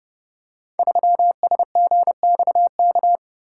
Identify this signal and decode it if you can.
Morse code, heard as a keyed tone, about 28 wpm, 700 Hz. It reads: 3SGXK
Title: Code Groups - 5 Letter:Numbers - 30wpm